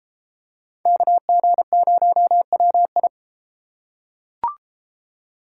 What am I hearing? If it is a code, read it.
Morse code, heard as a keyed tone, about 29 wpm, 700 Hz. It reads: KG0WI